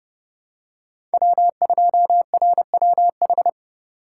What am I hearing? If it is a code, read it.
Morse code, heard as a keyed tone, about 30 wpm, 700 Hz. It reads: W2RWH
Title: USA Call Signs - 5 Characters - 4x Character Spacing - 30wpm